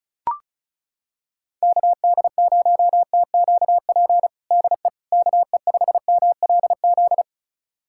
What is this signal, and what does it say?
Morse code, heard as a keyed tone, about 36 wpm, 700 Hz. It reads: KD0TQP DE KE5MLZ
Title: USA Callsign DE Callsign - 6 Characters - 35wpm